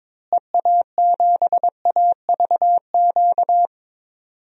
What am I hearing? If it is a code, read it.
Morse code, heard as a keyed tone, about 22 wpm, 700 Hz. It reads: EA7AVQ
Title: DXCC-100 Call Signs - 6 Characters - Repeated 3x - 22wpm